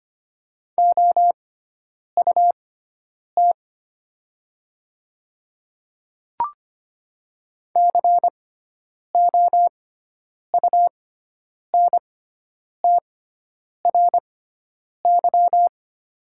Morse code, heard as a keyed tone, about 25 wpm, spaced as for 9 wpm, 700 Hz. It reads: OUT COUNTRY
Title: 6x Character Spacing - Top 300 Words - 25wpm